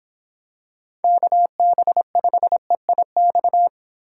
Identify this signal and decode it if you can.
Morse code, heard as a keyed tone, about 25 wpm, 700 Hz. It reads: KB5EIX